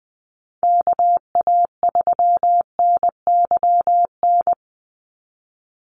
Morse code, heard as a keyed tone, about 20 wpm, 700 Hz. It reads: KA3NYN
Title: USA Call Signs 20wpm